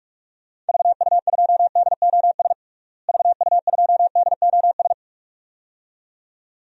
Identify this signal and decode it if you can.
Morse code, heard as a keyed tone, about 45 wpm, 700 Hz. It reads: VU2DOS VU2DOS
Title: DXCC-100 Call Signs - 6 Characters - Repeated 2x - 45wpm